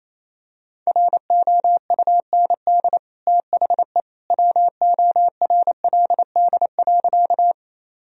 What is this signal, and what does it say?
Morse code, heard as a keyed tone, about 28 wpm, 700 Hz. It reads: ROUND THE WORLD.
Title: Sentences from Top 1000 Words - Review Part 1 - 28wpm